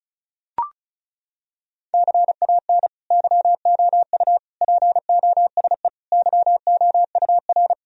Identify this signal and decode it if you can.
Morse code, heard as a keyed tone, about 34 wpm, 700 Hz. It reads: CAN YOU POSE YOUR
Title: Sentences from Top 1000 Words - Review Part 2 - 35wpm